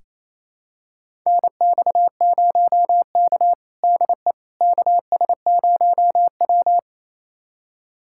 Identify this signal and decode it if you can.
Morse code, heard as a keyed tone, about 28 wpm, 700 Hz. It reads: NX0K DE KS0W
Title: USA Callsign DE Callsign - 4 Characters - 28wpm